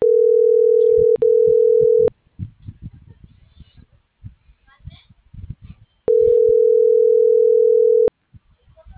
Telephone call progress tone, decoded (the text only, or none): ringback tone